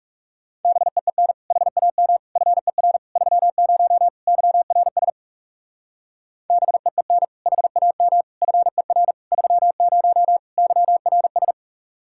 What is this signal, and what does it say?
Morse code, heard as a keyed tone, about 38 wpm, 700 Hz. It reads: BEEN HAM FER 30 YRS BEEN HAM FER 30 YRS